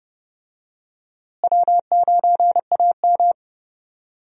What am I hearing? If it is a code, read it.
Morse code, heard as a keyed tone, about 30 wpm, 700 Hz. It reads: W9AM
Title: USA Call Signs - 4 Characters - Repeated 3x - 30wpm